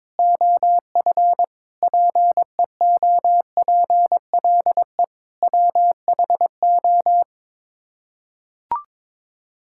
Morse code, heard as a keyed tone, about 22 wpm, 700 Hz. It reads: OF PEOPLE WHO
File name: practice / 22wpm / Sets of 7 Words - 22wpm